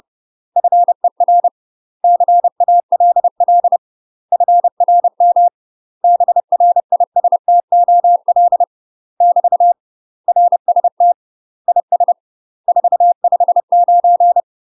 Morse code, heard as a keyed tone, about 30 wpm, 700 Hz. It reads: FER CALL FRM BRISTOL <BT> RST IS 459
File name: UK MASTER QSOs TRAINING / 30WPM / UK MASTER QSOs 0040